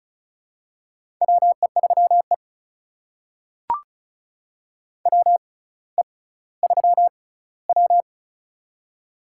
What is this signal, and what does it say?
Morse code, heard as a keyed tone, about 34 wpm, spaced as for 18 wpm, 700 Hz. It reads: WE3E WE3W
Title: USA Call Signs - 4 Characters - 6x Character Spacing - 35wpm